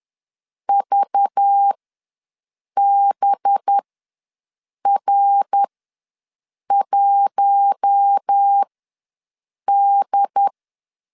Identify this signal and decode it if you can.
Morse code, heard as a keyed tone, about 10 wpm, spaced as for 6 wpm, 800 Hz. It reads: VBR1D
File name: LEZIONI / 5 LBVC / 5 LBV ws 10wpm 3645